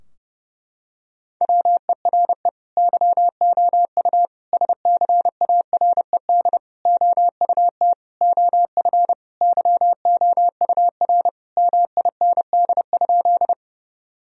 Morse code, heard as a keyed tone, about 30 wpm, 700 Hz. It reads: WERE YOU SCARED OUT OF YOUR MIND?
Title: Sentences from Top 1000 Words - Review Part 4 - 30wpm